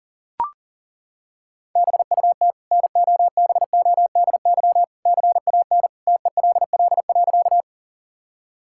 Morse code, heard as a keyed tone, about 40 wpm, 700 Hz. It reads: BUT NOBODY CAN TELL.
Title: Sentences from Top 1600 Words 40wpm